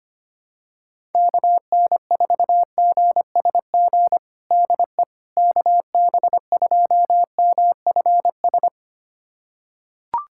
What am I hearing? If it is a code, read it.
Morse code, heard as a keyed tone, about 25 wpm, 700 Hz. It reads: KN4GSG DE KB2MFS
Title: USA Callsign DE Callsign - 6 Characters - 25wpm